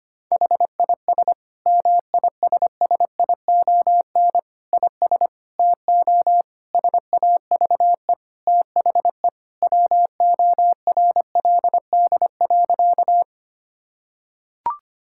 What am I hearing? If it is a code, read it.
Morse code, heard as a keyed tone, about 25 wpm, 700 Hz. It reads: HIS MISSION IS TO SAVE THE WORLD.
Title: Sentences from Top 1600 Words 25wpm